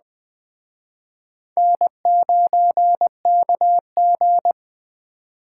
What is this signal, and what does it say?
Morse code, heard as a keyed tone, about 19 wpm, 700 Hz. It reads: N9KG